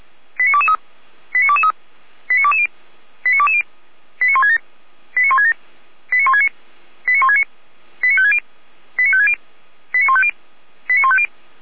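Five-tone selective call (ZVEI), heard as quick runs of five tones, tones of about 70 ms. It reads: 89223 89223 89220 89220 89167 89167 89169 89169 89560 89560 89150 89150
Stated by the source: Five-tone ZVEI selective call 89010 received by SDR, archived by Feuerwehr Orlamünde fire brigade.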